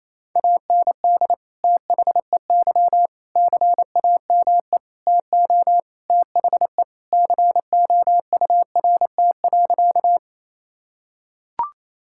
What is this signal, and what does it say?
Morse code, heard as a keyed tone, about 28 wpm, 700 Hz. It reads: AND THEY CAME TO THE COURT.
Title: Sentences from Top 1600 Words 28wpm